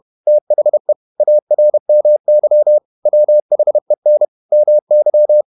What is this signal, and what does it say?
Morse code, heard as a keyed tone, about 31 wpm, 600 Hz. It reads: THE ARMY WHEN MY
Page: Never 31wpm | CW med Gnister